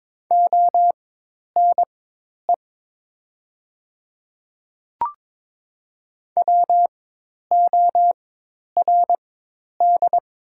Morse code, heard as a keyed tone, about 22 wpm, spaced as for 10 wpm, 700 Hz. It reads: ONE WORD
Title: 4x Character Spacing - Top 100 Words - 22wpm